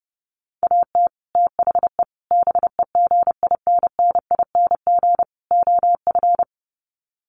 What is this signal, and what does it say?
Morse code, heard as a keyed tone, about 30 wpm, 700 Hz. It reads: AT THE BEGINNING OF
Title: Sets of 4 Words 30wpm